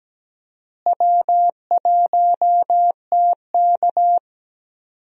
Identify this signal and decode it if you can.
Morse code, heard as a keyed tone, about 17 wpm, 700 Hz. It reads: W1TK
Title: USA Call Signs - 4 Characters - Repeated 2x - 17wpm